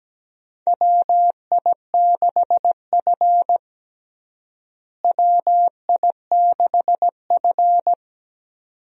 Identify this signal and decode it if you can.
Morse code, heard as a keyed tone, about 17 wpm, 700 Hz. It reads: WI6F WI6F
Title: USA Call Signs - 4 Characters - Repeated 2x - 17wpm